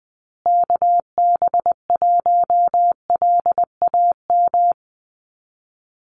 Morse code, heard as a keyed tone, about 20 wpm, 700 Hz. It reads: KB1LAM
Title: USA Call Signs Repeated 3x 20wpm